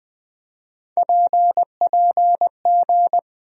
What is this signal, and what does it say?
Morse code, heard as a keyed tone, about 21 wpm, 700 Hz. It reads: PPG
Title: USA Call Sign Postfixes - 3 Letters 20wpm